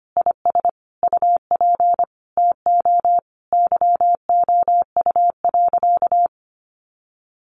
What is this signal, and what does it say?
Morse code, heard as a keyed tone, about 25 wpm, 700 Hz. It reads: IS UP TO YOU.
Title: Sentences from Top 1200 Words 25wpm